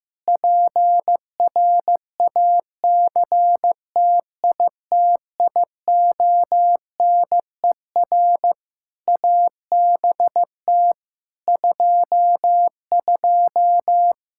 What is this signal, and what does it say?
Morse code, heard as a keyed tone, about 15 wpm, 700 Hz. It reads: PRACTITIONER ABT 22
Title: Practice QSO - Exchange 3 - More Info - 15wpm